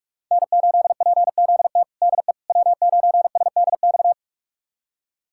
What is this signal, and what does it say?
Morse code, heard as a keyed tone, about 41 wpm, 700 Hz. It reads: N8PZT DE W9SDX
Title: USA Callsign DE Callsign - 5 Characters - 45wpm